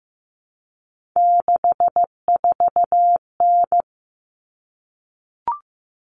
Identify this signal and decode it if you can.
Morse code, heard as a keyed tone, about 15 wpm, 700 Hz. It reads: 64N
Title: Common Cut Numbers - 3-Digit 15wpm